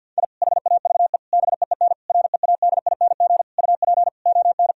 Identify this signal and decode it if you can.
Morse code, heard as a keyed tone, about 50 wpm, 700 Hz. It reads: I HAVE BEEN READING UP ON
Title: Sentences from Top 1000 Words - Review Part 3 - 50wpm